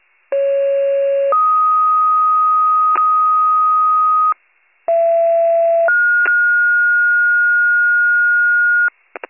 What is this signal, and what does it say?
Two-tone page, heard as a one-second tone, then a three-second tone, about 550 then 1250 Hz, about 670 then 1430 Hz, with noise. It is heard from 0.3 to 8.9 s.